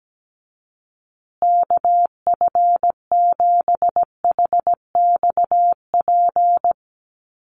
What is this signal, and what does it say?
Morse code, heard as a keyed tone, about 17 wpm, 700 Hz. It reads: KF7HXP
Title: USA Call Signs Repeated 3x 17wpm